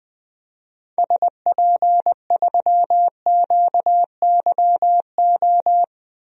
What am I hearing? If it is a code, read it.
Morse code, heard as a keyed tone, about 21 wpm, 700 Hz. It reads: SP3QYO
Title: DXCC-100 Call Signs - 6 Characters - Repeated 3x - 20wpm